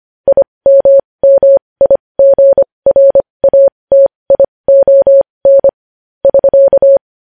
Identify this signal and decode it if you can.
Morse code, heard as a keyed tone, about 25 wpm, spaced as for 20 wpm, 550 Hz. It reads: IMMIGRATION <SK>
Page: news-headlines-in-morse-code-at-20-wpm